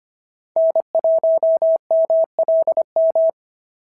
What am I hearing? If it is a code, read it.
Morse code, heard as a keyed tone, about 24 wpm, 650 Hz. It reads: N1MLM
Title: SOTA QSO Practice 25wpm